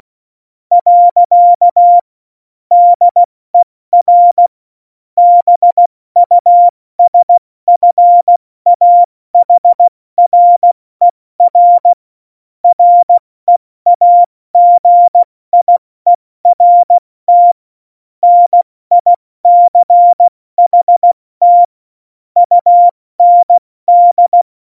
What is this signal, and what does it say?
Morse code, heard as a keyed tone, about 16 wpm, spaced as for 14 wpm, 700 Hz. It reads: . DER BUSFAHRER REAGIERT NICHT UND
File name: Morsekurs / Witze / Witz 29. Januar 2019, 70 F80